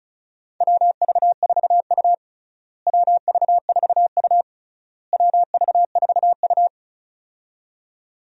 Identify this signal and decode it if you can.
Morse code, heard as a keyed tone, about 36 wpm, 700 Hz. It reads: WV4U WV4U WV4U
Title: USA Call Signs - 4 Characters - Repeated 3x - 35wpm